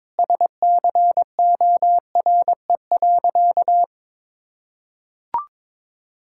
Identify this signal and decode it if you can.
Morse code, heard as a keyed tone, about 22 wpm, 700 Hz. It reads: SCORE.
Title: Sentences from Top 1600 Words 22wpm